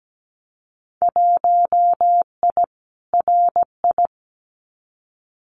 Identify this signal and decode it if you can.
Morse code, heard as a keyed tone, about 17 wpm, 700 Hz. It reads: 1I RI
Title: Winter Field Day Exchanges Encore 17wpm